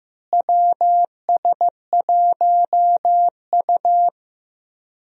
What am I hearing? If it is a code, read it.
Morse code, heard as a keyed tone, about 15 wpm, 700 Hz. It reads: WS1U
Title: USA Call Signs - 4 Characters - Repeated 2x - 15wpm